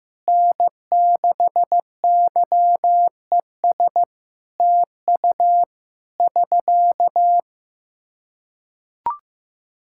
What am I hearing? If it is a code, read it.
Morse code, heard as a keyed tone, about 15 wpm, 700 Hz. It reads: N6YES TU <SK>
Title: Practice QSO - Exchange 4 - Ending - 15wpm